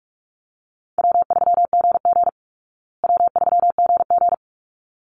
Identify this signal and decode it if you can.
Morse code, heard as a keyed tone, about 44 wpm, 700 Hz. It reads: W3GG W3GG
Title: Call Signs Repeated 3x - 45wpm